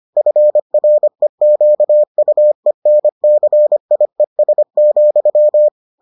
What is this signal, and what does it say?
Morse code, heard as a keyed tone, about 25 wpm, 600 Hz. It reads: FREQUENCIES,